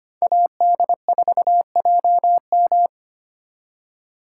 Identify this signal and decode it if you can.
Morse code, heard as a keyed tone, about 32 wpm, 700 Hz. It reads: AD4JM